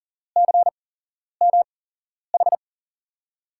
Morse code, heard as a keyed tone, about 40 wpm, spaced as for 11 wpm, 700 Hz. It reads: CMH
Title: USA Call Signs - 5 Characters - 8x Character Spacing - 40wpm